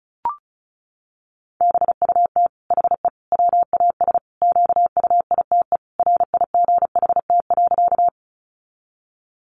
Morse code, heard as a keyed tone, about 34 wpm, 700 Hz. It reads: BUT HE WAS QUITE RIGHT.
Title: Sentences from Top 700 Words 35wpm